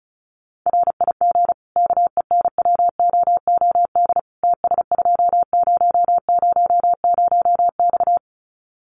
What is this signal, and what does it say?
Morse code, heard as a keyed tone, about 34 wpm, 700 Hz. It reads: RIG KENWOOD TS2000X
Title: Common QSO Phrases 35wpm